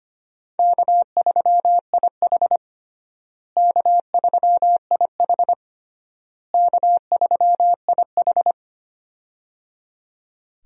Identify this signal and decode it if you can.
Morse code, heard as a keyed tone, about 25 wpm, 700 Hz. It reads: K3IH K3IH K3IH
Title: USA Call Signs - 4 Characters - Repeated 3x - 25wpm